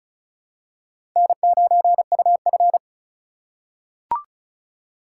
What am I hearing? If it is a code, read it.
Morse code, heard as a keyed tone, about 34 wpm, 690 Hz. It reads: N9UF
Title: USA Call Signs - 4 Characters - 2x Character Spacing - 35wpm